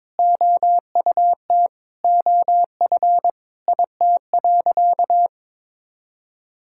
Morse code, heard as a keyed tone, about 21 wpm, 700 Hz. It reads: OUT OF IT.
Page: morse-code-ninja-practice-22wpm